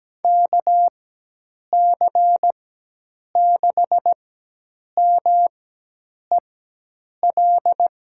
Morse code, heard as a keyed tone, about 17 wpm, spaced as for 8 wpm, 700 Hz. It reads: KC6MEL
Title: USA Call Signs - 6 Characters - 4x Character Spacing - 17wpm